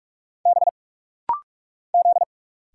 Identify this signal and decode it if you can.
Morse code, heard as a keyed tone, about 46 wpm, 700 Hz. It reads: B Z
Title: Single Letters - Rapid-Fire 45wpm